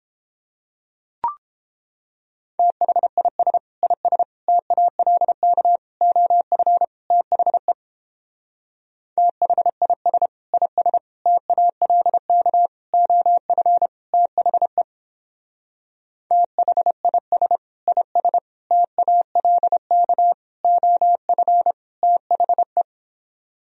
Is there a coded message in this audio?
Yes: Morse code from 2.6 to 22.8 s, heard as a keyed tone, about 29 wpm, 700 Hz. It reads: THIS IS TALK OF THE THIS IS TALK OF THE THIS IS TALK OF THE